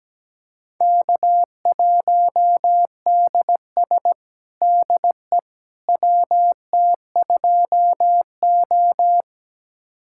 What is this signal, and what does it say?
Morse code, heard as a keyed tone, about 17 wpm, 700 Hz. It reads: K1DS DE WT2O
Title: USA Callsign DE Callsign - 4 Characters - 17wpm